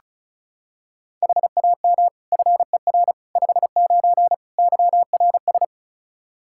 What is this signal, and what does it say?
Morse code, heard as a keyed tone, about 36 wpm, 700 Hz. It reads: HAM FER 59 YRS
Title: Practice QSO - Exchange 3 - More Info - 35wpm